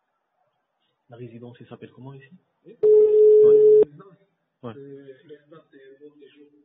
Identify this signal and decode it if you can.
Ringback tone (Europe, 425 Hz), with speech.